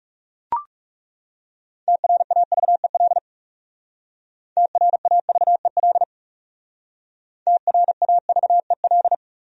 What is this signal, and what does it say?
Morse code, heard as a keyed tone, about 38 wpm, 700 Hz. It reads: TRAVEL TRAVEL TRAVEL
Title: Speed-Racing Top 500 Words 30wpm